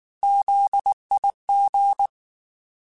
Morse code, heard as a keyed tone, about 19 wpm, 800 Hz. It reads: ZIG
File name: Alphabetical / Alphabetical mp3s / Zs @18